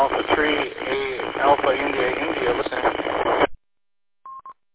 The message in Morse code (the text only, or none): none